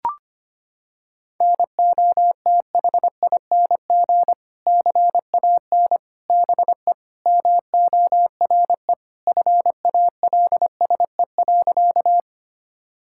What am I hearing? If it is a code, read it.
Morse code, heard as a keyed tone, about 25 wpm, 700 Hz. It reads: NOTHING CAN BE MORE FALSE.
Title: Sentences from Top 1600 Words 25wpm